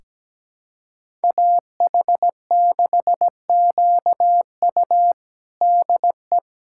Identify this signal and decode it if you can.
Morse code, heard as a keyed tone, about 17 wpm, 700 Hz. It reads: AH6QU DE
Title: USA Callsign DE Callsign - 5 Characters - 17wpm